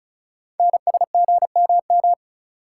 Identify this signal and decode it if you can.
Morse code, heard as a keyed tone, about 36 wpm, 690 Hz. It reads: NSGMM
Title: Code Groups - 5 Letters - 35wpm